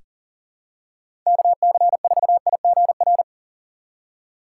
Morse code, heard as a keyed tone, about 40 wpm, 700 Hz. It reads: KC4IGR
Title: USA Call Signs - 6 Characters - 2x Character Spacing - 40wpm